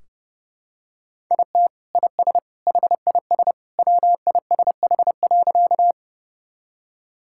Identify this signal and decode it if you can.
Morse code, heard as a keyed tone, about 30 wpm, 700 Hz. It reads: IT IS HIS WISH.
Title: Sentences from Top 1600 Words 30wpm